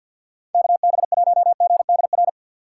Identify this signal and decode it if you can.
Morse code, heard as a keyed tone, about 53 wpm, 700 Hz. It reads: KB1GDR
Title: USA Call Signs - 6 Characters - 8x Character Spacing - 50wpm